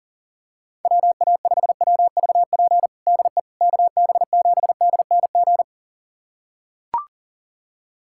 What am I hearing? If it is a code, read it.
Morse code, heard as a keyed tone, about 40 wpm, 700 Hz. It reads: WA5WVP DE KB7DNG
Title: USA Callsign DE Callsign - 6 Characters - 40wpm